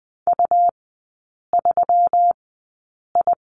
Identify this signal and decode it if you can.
Morse code, heard as a keyed tone, about 20 wpm, spaced as for 8 wpm, 700 Hz. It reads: U3I
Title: 127 Lesson 21e - Number 6 - Three Characters 20wpm